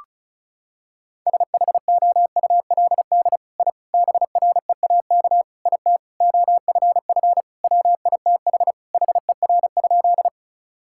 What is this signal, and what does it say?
Morse code, heard as a keyed tone, about 36 wpm, 700 Hz. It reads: SHOULD I BREAK IT OFF WITH HER?
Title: Sentences from Top 1000 Words - Review Part 6 - 35wpm